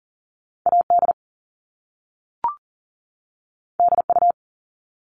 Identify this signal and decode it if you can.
Morse code, heard as a keyed tone, about 39 wpm, 700 Hz. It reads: AD DU
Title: Top Two Letter Combinations in Words 40wpm